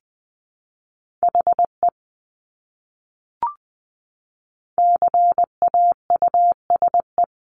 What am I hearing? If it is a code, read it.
Morse code, heard as a keyed tone, about 20 wpm, 700 Hz. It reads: HE CAUSE
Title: Top 200 Words Repeated 20wpm